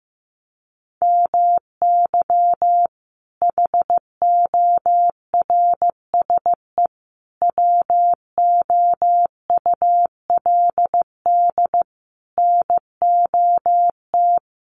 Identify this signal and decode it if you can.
Morse code, heard as a keyed tone, about 15 wpm, 700 Hz. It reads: MY HORSE WOULD NOT
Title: Sentences from Top 1400 Words 15wpm